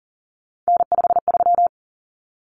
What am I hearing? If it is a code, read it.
Morse code, heard as a keyed tone, about 40 wpm, 700 Hz. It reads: N53
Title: Common Cut Numbers - 3-Digit 40wpm